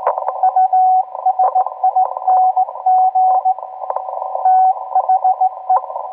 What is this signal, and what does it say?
Morse code, heard as a keyed tone, about 18 wpm, 750 Hz, with radio noise. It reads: USING THE